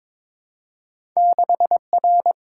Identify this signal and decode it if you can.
Morse code, heard as a keyed tone, about 23 wpm, 700 Hz. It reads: 6R
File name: practice / 22wpm / Two Letter-Number Combinations - 22wpm